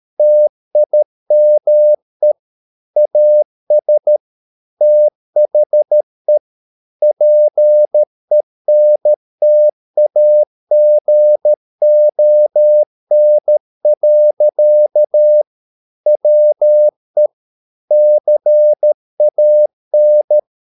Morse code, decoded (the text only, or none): TIME AS THE PENTAGON. WE CAN